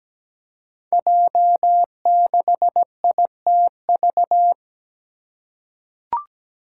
Morse code, heard as a keyed tone, about 17 wpm, 700 Hz. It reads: J6ITV
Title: Five Challenging Letter-Number Combinations 17wpm